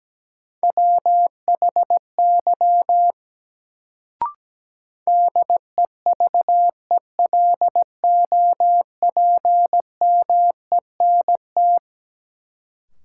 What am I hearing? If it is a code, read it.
Morse code, heard as a keyed tone, about 17 wpm, 700 Hz. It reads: WHY DEVELOPMENT
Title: Sets of 3 Words - New Word Drill - 17wpm